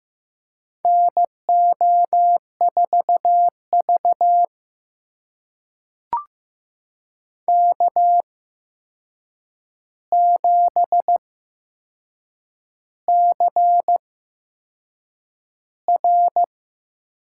Morse code, heard as a keyed tone, about 15 wpm, spaced as for 6 wpm, 700 Hz. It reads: NO4V K7CR
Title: USA Call Signs - 4 Characters - 8x Character Spacing - 15wpm